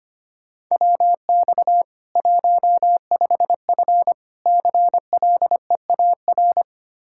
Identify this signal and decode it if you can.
Morse code, heard as a keyed tone, about 25 wpm, 700 Hz. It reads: WX 15F CLEAR
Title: Practice QSO - Exchange 2 - Info - 25wpm